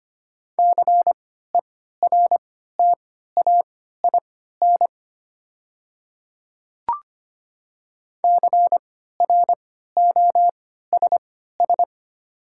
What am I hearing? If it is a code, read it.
Morse code, heard as a keyed tone, about 25 wpm, spaced as for 14 wpm, 690 Hz. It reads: CERTAIN CROSS
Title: 3x Character Spacing - Top 500 Words - 25wpm